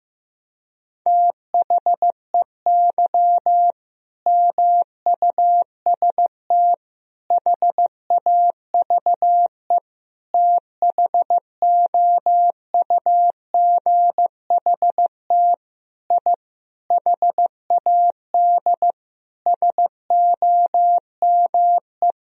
Morse code, heard as a keyed tone, about 15 wpm, 700 Hz. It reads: THEY MUST HAVE THOUGHT I HAD SOME
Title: Sentences from Top 1000 Words - Review Part 4 - 15wpm